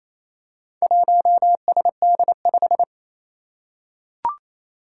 Morse code, heard as a keyed tone, about 29 wpm, 700 Hz. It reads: 1SD5
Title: Four Challenging Letter-Number Combinations 28wpm